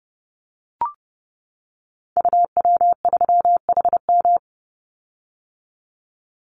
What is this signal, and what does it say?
Morse code, heard as a keyed tone, about 30 wpm, 700 Hz. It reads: UW3HM
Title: Call Signs - 30wpm